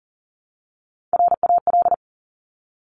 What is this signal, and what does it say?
Morse code, heard as a keyed tone, about 38 wpm, 700 Hz. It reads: RAL
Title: Top Three Letter Combinations in Words 40wpm